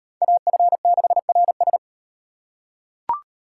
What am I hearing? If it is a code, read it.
Morse code, heard as a keyed tone, about 38 wpm, 700 Hz. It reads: AF6RS